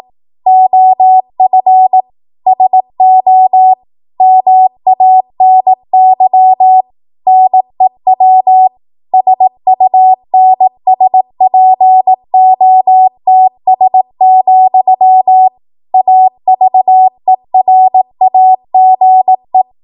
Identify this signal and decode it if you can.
Morse code, heard as a keyed tone, about 18 wpm, 750 Hz. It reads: OF SO MANY NEW SUNSPOTS, AVERAGE
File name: Archive / 18 WPM / 18WPM